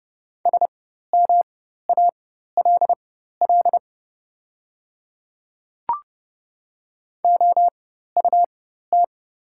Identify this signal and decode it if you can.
Morse code, heard as a keyed tone, about 30 wpm, spaced as for 14 wpm, 700 Hz. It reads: SMALL OUT
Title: 4x Character Spacing - Top 200 Words - 30wpm